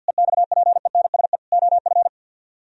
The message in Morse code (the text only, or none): EXPENSE OF